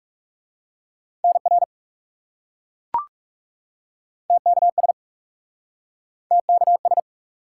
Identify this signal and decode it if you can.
Morse code, heard as a keyed tone, about 41 wpm, 700 Hz. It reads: NR TKS TKS